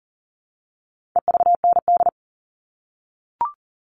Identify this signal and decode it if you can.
Morse code, heard as a keyed tone, about 40 wpm, 700 Hz. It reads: EVND
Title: All Cut Numbers - 4-Digit 40wpm